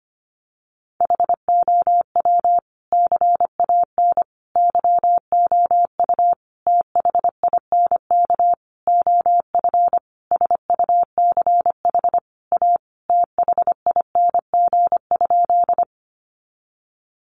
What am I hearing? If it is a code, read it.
Morse code, heard as a keyed tone, about 25 wpm, 700 Hz. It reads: HOW CAN YOU THINK OF SUCH A THING?
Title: Sentences from Top 200 Words 25wpm